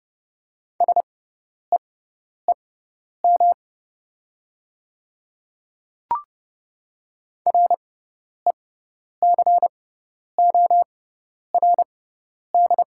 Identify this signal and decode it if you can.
Morse code, heard as a keyed tone, about 30 wpm, spaced as for 10 wpm, 690 Hz. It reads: SEEM RECORD
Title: 6x Character Spacing - Top 500 Words - 30wpm